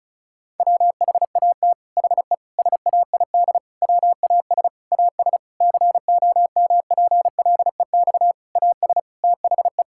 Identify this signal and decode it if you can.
Morse code, heard as a keyed tone, about 34 wpm, 700 Hz. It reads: WHAT HE SAID WAS AS COMPLEX AS THE
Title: Sentences from Top 1000 Words - Review Part 6 - 35wpm